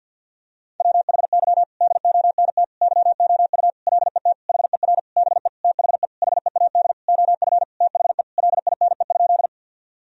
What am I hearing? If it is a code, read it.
Morse code, heard as a keyed tone, about 53 wpm, 700 Hz. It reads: WHY DONT YOU LET HER BE THE HEAD OF THE LINE?
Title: Sentences from Top 1000 Words - Review Part 1 - 50wpm